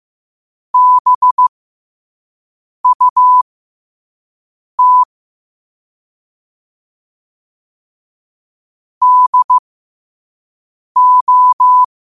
Morse code, heard as a keyed tone, about 15 wpm, spaced as for 5 wpm, 1000 Hz. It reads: BUT DO
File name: qotdfiles / 5wpm